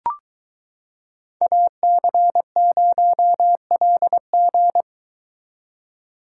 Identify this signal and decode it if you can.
Morse code, heard as a keyed tone, about 22 wpm, 700 Hz. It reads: AC0LG